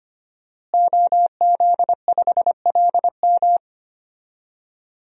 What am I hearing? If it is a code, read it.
Morse code, heard as a keyed tone, about 25 wpm, 700 Hz. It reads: OZ5LM
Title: DXCC-100 Call Signs - 5 Characters - Repeated 3x - 25wpm